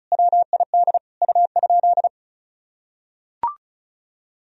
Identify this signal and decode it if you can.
Morse code, heard as a keyed tone, about 34 wpm, 700 Hz. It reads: WID U?
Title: Sentences from Top 100 Words and Abbreviations 35wpm